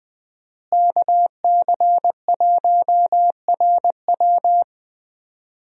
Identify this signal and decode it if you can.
Morse code, heard as a keyed tone, about 20 wpm, 700 Hz. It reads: KC1RW
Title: USA Call Signs - 5 Characters - Repeated 2x - 20wpm